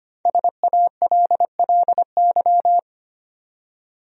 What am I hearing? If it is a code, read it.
Morse code, heard as a keyed tone, about 25 wpm, 700 Hz. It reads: SALLY
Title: USA Common Names - 25wpm